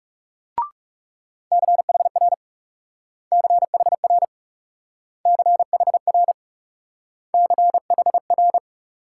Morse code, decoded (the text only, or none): CHR CHR CHR CHR